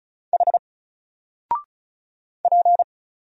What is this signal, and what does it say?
Morse code, heard as a keyed tone, about 35 wpm, 700 Hz. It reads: H P
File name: Single-Letter-Number-35wpm